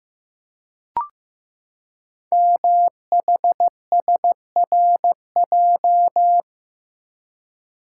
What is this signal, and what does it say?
Morse code, heard as a keyed tone, about 15 wpm, 700 Hz. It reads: MHSRJ
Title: Code Groups - 5 Letters - 15wpm